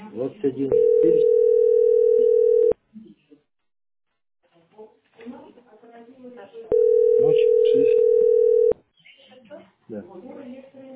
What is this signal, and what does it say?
Ringback tone (North America), 440 + 480 Hz, with speech.
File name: ce6f425e-19e6-43f5-a331